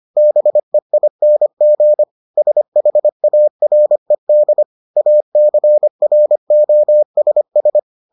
Morse code, heard as a keyed tone, about 25 wpm, 600 Hz. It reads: BEING SHARED ACROSS